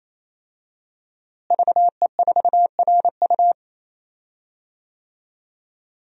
Morse code, heard as a keyed tone, about 28 wpm, 700 Hz. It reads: VE4RU
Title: DXCC-100 Call Signs - 5 Characters - Repeated 3x - 28wpm